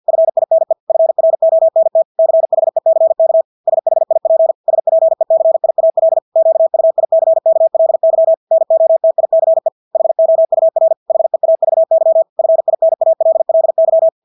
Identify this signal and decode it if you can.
Morse code, heard as a keyed tone, about 50 wpm, 650 Hz. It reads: FINE FRONT CHECK SHIP SPECIAL QUICKLY NOTICE HOUR HEAVY FINALLY
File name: Engelske ord 50 wpm